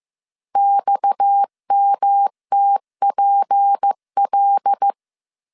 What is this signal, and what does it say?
Morse code, heard as a keyed tone, about 15 wpm, 800 Hz. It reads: XMTPL
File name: CW / DET / 1 6 1500C S C 14wpm 2951